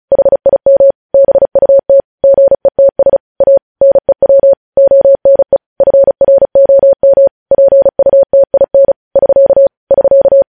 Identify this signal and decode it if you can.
Morse code, heard as a keyed tone, about 36 wpm, 550 Hz. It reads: HIM BUT GETS A NEW ONE FROM PUTIN <SK> <SK>
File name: podcast / News headlines-1x-35x35